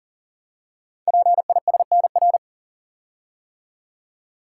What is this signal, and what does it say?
Morse code, heard as a keyed tone, about 42 wpm, 690 Hz. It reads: PISNR
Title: Code Groups - 5 Letters - 40wpm